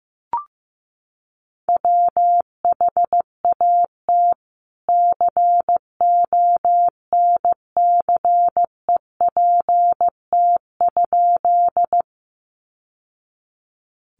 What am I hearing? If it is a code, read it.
Morse code, heard as a keyed tone, about 14 wpm, 700 Hz. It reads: WHAT CONCEPT?
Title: Sentences from Top 1100 Words 15wpm